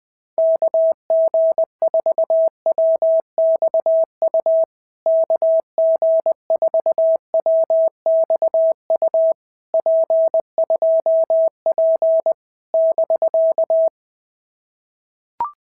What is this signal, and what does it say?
Morse code, heard as a keyed tone, about 19 wpm, 650 Hz. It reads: KG4WXU KG4WXU P2P <BK>
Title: POTA QSO Practice - 20wpm